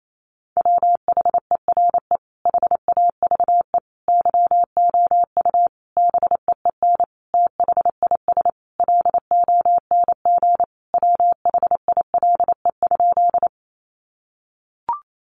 Morse code, heard as a keyed tone, about 28 wpm, 700 Hz. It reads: WHERE HAVE YOU BEEN THIS LONG WHILE?
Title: Sentences from Top 300 Words 28wpm